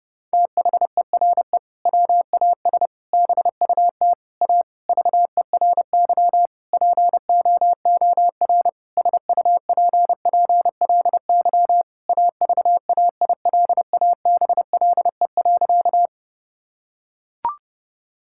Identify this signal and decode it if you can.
Morse code, heard as a keyed tone, about 30 wpm, 700 Hz. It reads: THERE WAS BUT A VERY POOR SUPPLY AVAILABLE.
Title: Sentences from Top 1000 Words - Review Part 5 - 30wpm